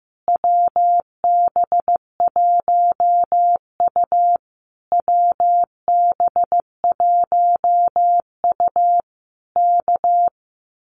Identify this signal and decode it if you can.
Morse code, heard as a keyed tone, about 15 wpm, 700 Hz. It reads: WB1U WB1U K
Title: Common QSO Phrases Encore 15wpm